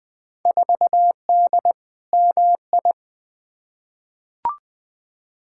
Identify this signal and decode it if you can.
Morse code, heard as a keyed tone, about 20 wpm, 700 Hz. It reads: 4D MI
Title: Field Day Exchanges Encore - 20wpm